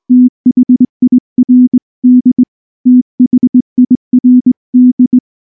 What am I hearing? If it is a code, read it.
Morse code, heard as a keyed tone, about 19 wpm, 250 Hz. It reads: THIRD THIRD
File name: morse / morse-training-20WPM-3